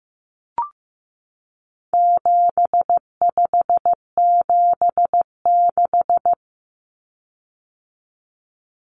Morse code, heard as a keyed tone, about 15 wpm, 700 Hz. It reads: 7576
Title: Common Cut Numbers - 4-Digit 15wpm